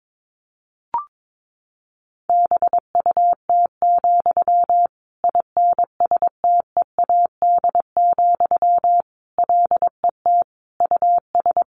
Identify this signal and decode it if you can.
Morse code, heard as a keyed tone, about 22 wpm, 700 Hz. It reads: BUT, INSTEAD, LET US
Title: Sentences from Top 1500 Words 22wpm